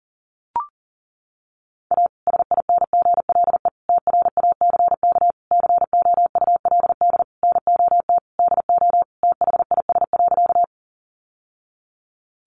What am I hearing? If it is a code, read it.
Morse code, heard as a keyed tone, about 37 wpm, 700 Hz. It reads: A SINGLE TRACK COULD NOT DO THIS.
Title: Sentences from Top 1100 Words 40wpm